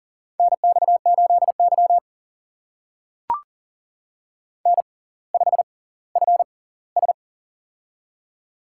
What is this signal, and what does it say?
Morse code, heard as a keyed tone, about 40 wpm, spaced as for 21 wpm, 700 Hz. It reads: NX8Y N5FS
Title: USA Call Signs - 4 Characters - 6x Character Spacing - 40wpm